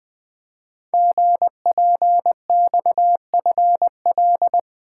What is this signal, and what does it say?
Morse code, heard as a keyed tone, about 20 wpm, 700 Hz. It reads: GPXFL
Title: Code Groups - 5 Letters - 20wpm